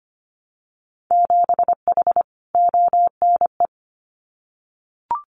Morse code, heard as a keyed tone, about 25 wpm, 700 Hz. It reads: 7H ONE
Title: Winter Field Day Exchanges Encore 25wpm